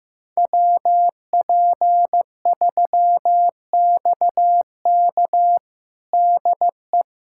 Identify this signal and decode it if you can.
Morse code, heard as a keyed tone, about 15 wpm, 700 Hz. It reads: WP3XK DE
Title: USA Callsign DE Callsign - 5 Characters - 15wpm